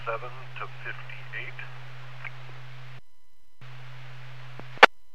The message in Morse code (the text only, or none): none